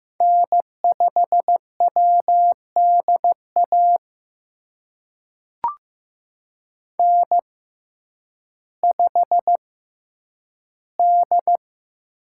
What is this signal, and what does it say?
Morse code, heard as a keyed tone, about 15 wpm, spaced as for 9 wpm, 700 Hz. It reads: N5WDA N5D
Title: USA Call Signs - 5 Characters - 6x Character Spacing - 15wpm